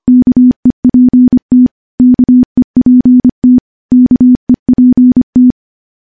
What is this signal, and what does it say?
Morse code, heard as a keyed tone, about 25 wpm, 250 Hz. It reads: KEPT KEPT KEPT
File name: morse / morse-training-25WPM